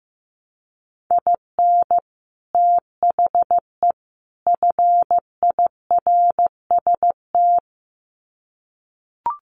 Morse code, heard as a keyed tone, about 15 wpm, 700 Hz. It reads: IN THE FIRST
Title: Sets of 3 Words 15wpm